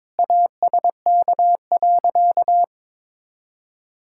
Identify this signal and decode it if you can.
Morse code, heard as a keyed tone, about 22 wpm, 700 Hz. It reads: ASK.
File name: practice / 22wpm / Sentences with 8-Word Phrases - 22wpm